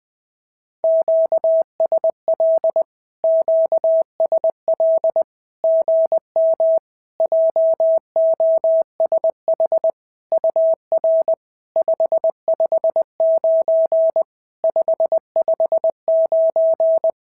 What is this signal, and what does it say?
Morse code, heard as a keyed tone, about 20 wpm, 650 Hz. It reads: QSL QSL GM JOSH UR 559 559
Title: SOTA QSO Practice 20wpm